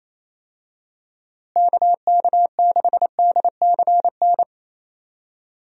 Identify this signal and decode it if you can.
Morse code, heard as a keyed tone, about 28 wpm, 700 Hz. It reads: KK6DCN
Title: USA Call Signs - 6 Characters - Repeated 3x - 28wpm